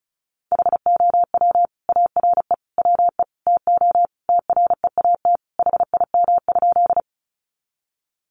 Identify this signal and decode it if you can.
Morse code, heard as a keyed tone, about 34 wpm, 700 Hz. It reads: HOW ARE WE TO TREAT HIM?
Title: Sentences from Top 1300 Words 35wpm